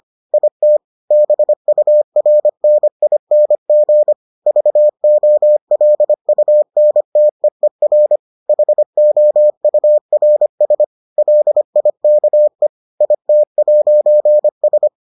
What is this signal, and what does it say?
Morse code, heard as a keyed tone, about 25 wpm, 600 Hz. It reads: IT BURNING VOLUNTEER HOURS LIKE IT'S